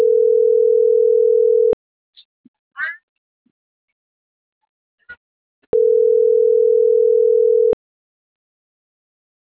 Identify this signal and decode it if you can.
Ringback tone (North America), 440 + 480 Hz.